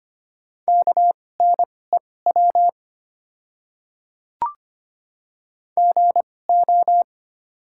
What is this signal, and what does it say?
Morse code, heard as a keyed tone, about 25 wpm, spaced as for 18 wpm, 690 Hz. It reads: KNEW GO
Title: 2x Character Spacing - Top 500 Words - 25wpm